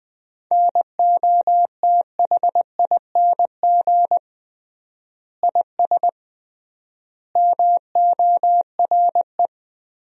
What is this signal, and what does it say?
Morse code, heard as a keyed tone, about 20 wpm, 700 Hz. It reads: NOTHING IS MORE
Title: 3x Word-Spacing - Sentences from Top 500 Words - 20wpm